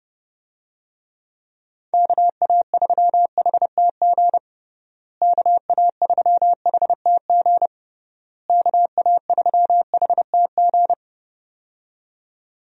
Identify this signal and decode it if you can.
Morse code, heard as a keyed tone, about 30 wpm, 700 Hz. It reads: KA3HTG KA3HTG KA3HTG
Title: USA Call Signs - 6 Characters - Repeated 3x - 30wpm